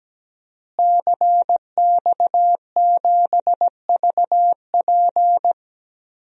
Morse code, heard as a keyed tone, about 17 wpm, 700 Hz. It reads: CX7VP
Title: DXCC-100 Call Signs - 5 Characters - Repeated 2x - 17wpm